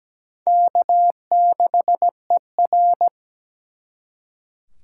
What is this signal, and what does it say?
Morse code, heard as a keyed tone, about 20 wpm, 700 Hz. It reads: K6ER
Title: Speed Racing - USA Call Signs 17wpm